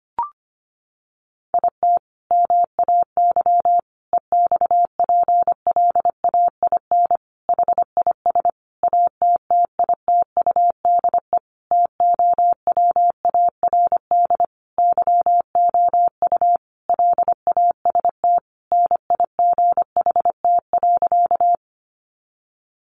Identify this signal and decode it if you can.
Morse code, heard as a keyed tone, about 25 wpm, 700 Hz. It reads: IT MAY EXPLAIN HIS ATTITUDE TOWARD YOU LAST NIGHT.
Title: Sentences from Top 1300 Words 25wpm